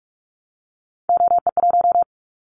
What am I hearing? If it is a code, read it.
Morse code, heard as a keyed tone, about 46 wpm, 700 Hz. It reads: OE1
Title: Callsign Prefixes for Top 50 DXCC - 45wpm